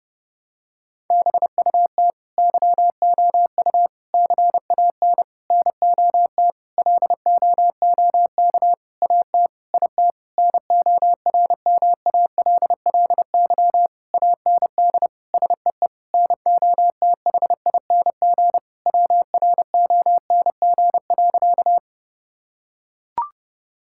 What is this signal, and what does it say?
Morse code, heard as a keyed tone, about 29 wpm, 700 Hz. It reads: BUT YOU CAN NOT LOOK AT IT NORMALLY AND SEE NOTHING WRONG.
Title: Sentences from Top 1000 Words - Review Part 6 - 30wpm